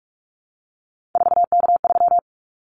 Morse code, heard as a keyed tone, about 44 wpm, 700 Hz. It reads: 4K3
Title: Callsign Prefixes for Top 100 DXCC - 45wpm